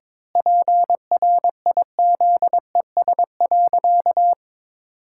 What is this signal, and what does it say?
Morse code, heard as a keyed tone, about 22 wpm, 700 Hz. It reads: PRIZES.
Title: Pangram Challenge v2 - 22wpm